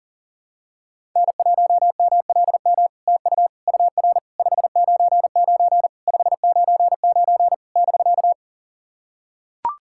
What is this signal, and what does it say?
Morse code, heard as a keyed tone, about 40 wpm, 700 Hz. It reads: N1MLM TU UR 599 599 <BK>
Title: SOTA QSO Practice 40wpm